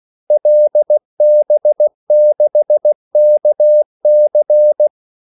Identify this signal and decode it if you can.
Morse code, heard as a keyed tone, about 16 wpm, 600 Hz. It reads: LB6KC
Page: Kaldesignaler 16 wpm